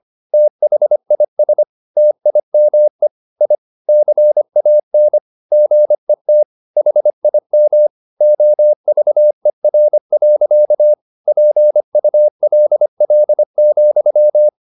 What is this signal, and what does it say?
Morse code, heard as a keyed tone, about 25 wpm, 600 Hz. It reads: THIS TIME I CAN GET HIM OVER. PULL,